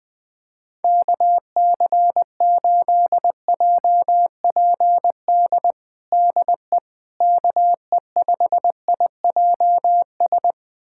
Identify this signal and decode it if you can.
Morse code, heard as a keyed tone, about 21 wpm, 700 Hz. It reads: KC8JPD DE KE5IJS
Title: USA Callsign DE Callsign - 6 Characters - 20wpm